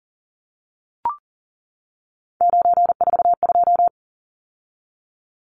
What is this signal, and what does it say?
Morse code, heard as a keyed tone, about 40 wpm, 700 Hz. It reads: 942
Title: Numbers - 3-Digit 40wpm